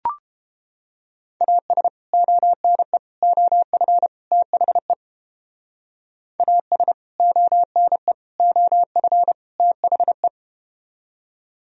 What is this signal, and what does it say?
Morse code, heard as a keyed tone, about 32 wpm, 700 Hz. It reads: AS ONE OF THE AS ONE OF THE